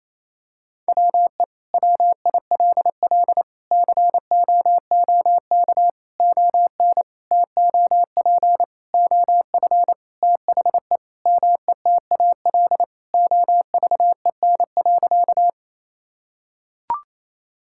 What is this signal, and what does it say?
Morse code, heard as a keyed tone, about 27 wpm, 700 Hz. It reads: WE WILL COOK ON TOP OF THE METAL OVEN.
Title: Sentences from Top 1000 Words - Review Part 6 - 28wpm